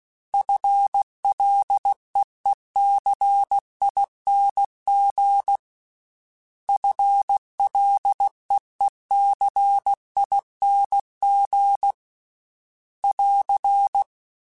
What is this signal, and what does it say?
Morse code, heard as a keyed tone, about 16 wpm, 800 Hz. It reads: FLEECING FLEECING <AR>
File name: FL / FL mp3s / FL Prefixes 3x2 @15